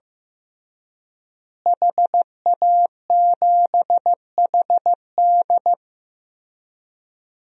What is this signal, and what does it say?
Morse code, heard as a keyed tone, about 15 wpm, 700 Hz. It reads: HA7HD
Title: DXCC-100 Call Signs - 5 Characters - Repeated 3x - 15wpm